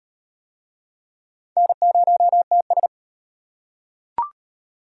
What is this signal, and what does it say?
Morse code, heard as a keyed tone, about 38 wpm, 700 Hz. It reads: N0TS